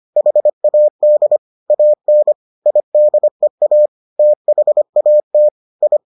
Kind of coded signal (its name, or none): Morse code